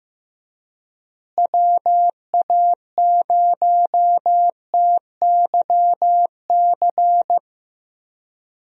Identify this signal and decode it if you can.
Morse code, heard as a keyed tone, about 15 wpm, 700 Hz. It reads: WA0TYC
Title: USA Call Signs - 6 Characters - Repeated 3x - 15wpm